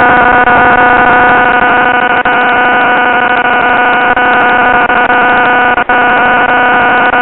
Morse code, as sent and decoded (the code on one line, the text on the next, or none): none